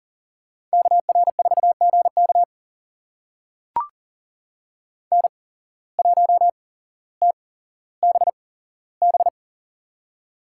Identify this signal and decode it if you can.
Morse code, heard as a keyed tone, about 40 wpm, spaced as for 17 wpm, 700 Hz. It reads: KR4GK N1TBB
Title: USA Call Signs - 5 Characters - 8x Character Spacing - 40wpm